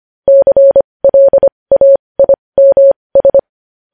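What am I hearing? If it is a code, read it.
Morse code, heard as a keyed tone, about 25 wpm, spaced as for 20 wpm, 550 Hz. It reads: CLAIMS
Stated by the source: Morse code sending news headlines at 20 wpm.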